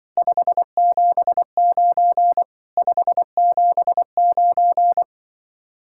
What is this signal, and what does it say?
Morse code, heard as a keyed tone, about 25 wpm, 700 Hz. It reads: 579 579
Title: Speed-Racing Common QSO Phrases Encore - 20wpm